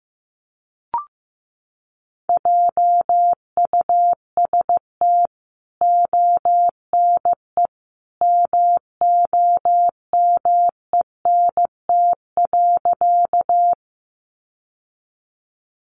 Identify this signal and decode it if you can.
Morse code, heard as a keyed tone, about 15 wpm, 700 Hz. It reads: JUST ONE MOMENT.
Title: Sentences from Top 1000 Words 15wpm